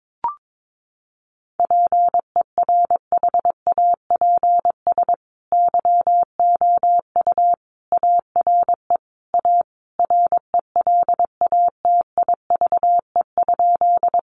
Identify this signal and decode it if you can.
Morse code, heard as a keyed tone, about 22 wpm, 700 Hz. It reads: PERHAPS YOU ARE A RELATIVE?
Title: Sentences from Top 1400 Words 22wpm